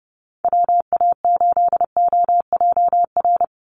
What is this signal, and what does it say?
Morse code, heard as a keyed tone, about 30 wpm, 700 Hz. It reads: WA8OJR
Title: Call Signs Repeated 2x - 30wpm